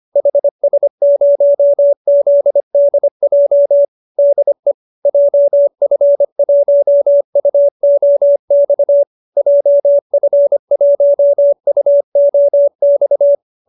Morse code, heard as a keyed tone, about 25 wpm, 590 Hz. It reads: HS0ZDJ DE JF1UOX JF1UOX